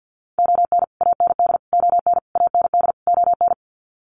Morse code, heard as a keyed tone, about 50 wpm, 700 Hz. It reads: ON AND ON AND ON
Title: Sets of 5 Words Encore2 50wpm